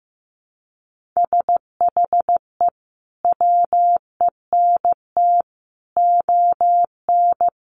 Morse code, heard as a keyed tone, about 15 wpm, 700 Hz. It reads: SHE WENT ON
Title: Sentences from Top 800 Words 15wpm